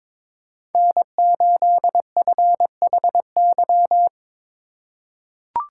Morse code, heard as a keyed tone, about 22 wpm, 700 Hz. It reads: N8FHY
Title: USA Call Signs - 5 Characters - 4x Character Spacing - 22wpm